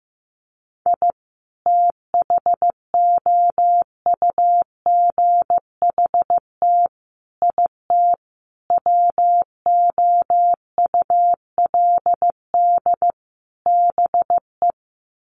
Morse code, heard as a keyed tone, about 15 wpm, 700 Hz. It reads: I THOUGHT IT WOULD BE
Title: Sets of 5 Words Encore 15wpm